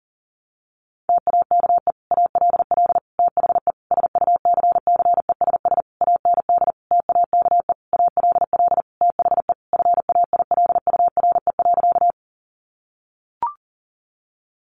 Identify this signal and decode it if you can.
Morse code, heard as a keyed tone, about 40 wpm, 700 Hz. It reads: TAKE ALL THE SUCCESS AND TAKE ALL THE FAILURE.
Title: Sentences from Top 900 Words 40wpm